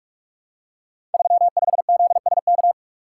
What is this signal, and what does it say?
Morse code, heard as a keyed tone, about 44 wpm, 700 Hz. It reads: 35ZSK
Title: Code Groups - 5 Letter:Numbers - 45wpm